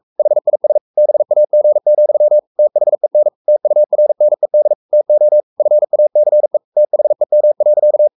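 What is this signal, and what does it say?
Morse code, heard as a keyed tone, about 44 wpm, 600 Hz. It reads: HIS BAG, THEN TURNED TO FACE THEM.
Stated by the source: Morse code at 43 wpm sending Ken Follett's novel Never in English, from CW med Gnister.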